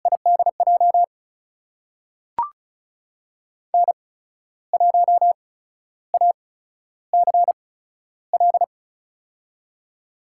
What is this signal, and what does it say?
Morse code, heard as a keyed tone, about 36 wpm, spaced as for 13 wpm, 700 Hz. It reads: IDJ N1ACL
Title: USA Call Signs - 8x Character Spacing - 35wpm